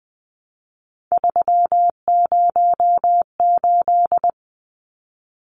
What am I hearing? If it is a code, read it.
Morse code, heard as a keyed tone, about 20 wpm, 700 Hz. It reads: 308
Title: Numbers - 3-Digit 20wpm